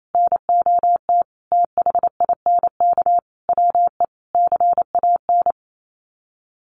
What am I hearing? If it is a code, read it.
Morse code, heard as a keyed tone, about 28 wpm, 700 Hz. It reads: NOT THINK WE CAN
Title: Sets of 5 Words Encore2 28wpm